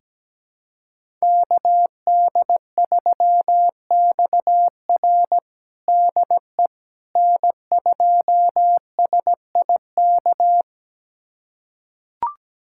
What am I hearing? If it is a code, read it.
Morse code, heard as a keyed tone, about 17 wpm, 700 Hz. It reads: KD3XR DE N2SIK
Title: USA Callsign DE Callsign - 5 Characters - 17wpm